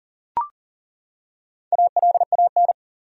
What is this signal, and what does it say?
Morse code, heard as a keyed tone, about 40 wpm, 700 Hz. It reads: ALAN
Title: USA Common Names - 40wpm